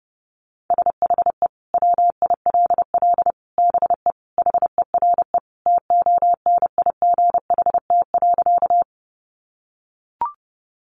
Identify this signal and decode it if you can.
Morse code, heard as a keyed tone, about 30 wpm, 700 Hz. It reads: SHE WILL BE HERE TONIGHT.
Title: Sentences from Top 600 Words 30wpm